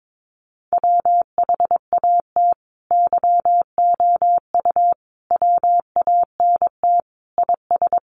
Morse code, heard as a keyed tone, about 22 wpm, 700 Hz. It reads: WHAT YOU WANT IS
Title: Sentences from Top 700 Words 22wpm